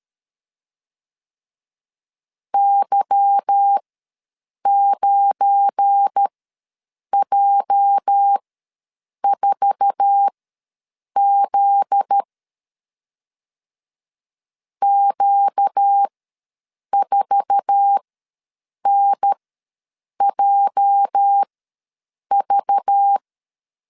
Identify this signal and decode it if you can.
Morse code, heard as a keyed tone, about 13 wpm, spaced as for 7 wpm, 800 Hz. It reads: Y9J4Z Q4NJV
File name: LEZIONI / 8 YJ45 / 8 YJ4 ws 12wpm 3249